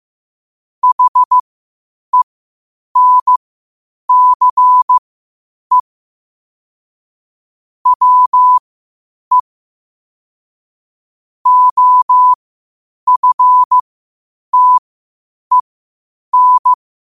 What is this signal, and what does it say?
Morse code, heard as a keyed tone, about 15 wpm, spaced as for 8 wpm, 1000 Hz. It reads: HENCE WE OFTEN
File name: qotdfiles / 8wpm